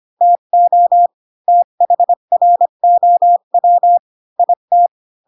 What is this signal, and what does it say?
Morse code, heard as a keyed tone, about 25 wpm, 700 Hz. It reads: TO THROW IT